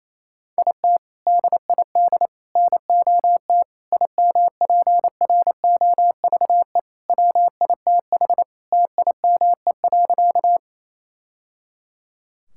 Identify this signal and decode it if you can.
Morse code, heard as a keyed tone, about 28 wpm, 700 Hz. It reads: IT DID NOT IMPROVE WITH TIME.
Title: Sentences from Top 1000 Words - Review Part 5 - 28wpm